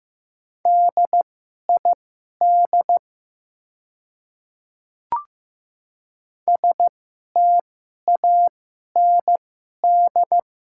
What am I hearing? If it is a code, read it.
Morse code, heard as a keyed tone, about 15 wpm, spaced as for 11 wpm, 700 Hz. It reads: DID STAND
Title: 2x Character Spacing - Top 300 Words - 15wpm